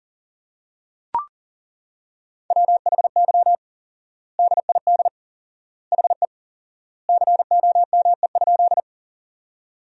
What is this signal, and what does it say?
Morse code, heard as a keyed tone, about 40 wpm, 700 Hz. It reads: WHY DID HE COME?
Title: 4x Word-Spacing - Sentences from Top 200 Words - 40wpm